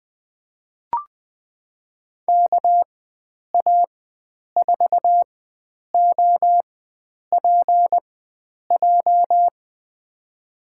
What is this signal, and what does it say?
Morse code, heard as a keyed tone, about 21 wpm, spaced as for 9 wpm, 700 Hz. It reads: KA4OPJ
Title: USA Call Signs - 6 Characters - 4x Character Spacing - 20wpm